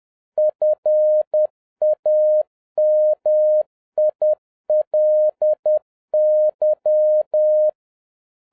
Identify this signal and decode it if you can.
Morse code, heard as a keyed tone, about 10 wpm, 610 Hz. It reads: FAMILY